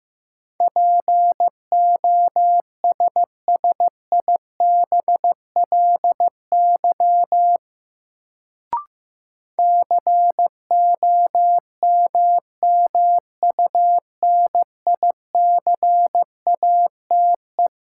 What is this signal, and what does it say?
Morse code, heard as a keyed tone, about 15 wpm, 700 Hz. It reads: POSSIBLY COMMUNICATE
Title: Sentences from Top 1300 Words - New Word Drill - 15wpm